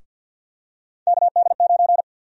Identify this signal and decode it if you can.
Morse code, heard as a keyed tone, about 53 wpm, 700 Hz. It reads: KD9
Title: USA Call Sign Prefixes - Repeated 2x - 50wpm